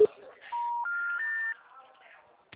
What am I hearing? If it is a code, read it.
Special information tone, with speech.
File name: out- -2017